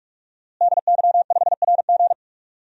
Morse code, heard as a keyed tone, about 42 wpm, 700 Hz. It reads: DY5RG
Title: DXCC-100 Call Signs - 5 Characters - Repeated 2x - 45wpm